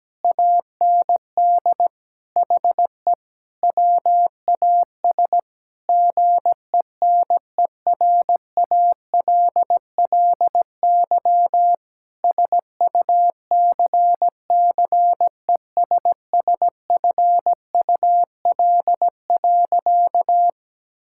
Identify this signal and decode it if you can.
Morse code, heard as a keyed tone, about 17 wpm, 700 Hz. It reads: AND HE WAS GENERALLY SUCCESSFUL.
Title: Sentences from Top 1000 Words - Review Part 6 - 17wpm